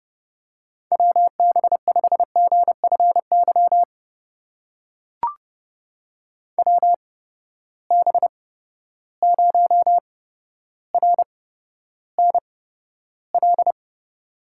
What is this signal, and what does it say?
Morse code, heard as a keyed tone, about 30 wpm, spaced as for 13 wpm, 700 Hz. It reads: WB5GFY WB0RNL
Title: USA Call Signs - 6 Characters - 8x Character Spacing - 30wpm